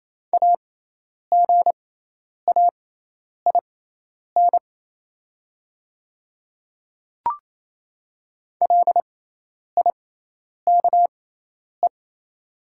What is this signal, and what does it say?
Morse code, heard as a keyed tone, about 27 wpm, spaced as for 10 wpm, 700 Hz. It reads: AGAIN LIKE
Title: 6x Character Spacing - Top 200 Words - 28wpm